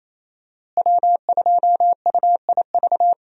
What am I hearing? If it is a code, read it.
Morse code, heard as a keyed tone, about 27 wpm, 700 Hz. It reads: W2UIV
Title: USA Call Signs - 5 Characters - Repeated 3x - 28wpm